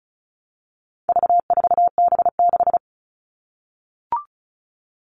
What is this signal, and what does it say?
Morse code, heard as a keyed tone, about 34 wpm, 700 Hz. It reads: V4B6
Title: All Cut Numbers - 4-Digit 35wpm